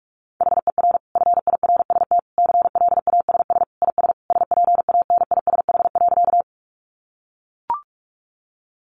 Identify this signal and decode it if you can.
Morse code, heard as a keyed tone, about 44 wpm, 700 Hz. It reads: HER FIRST CLASS IS SPANISH.
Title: Additional Easy Phrases 45wpm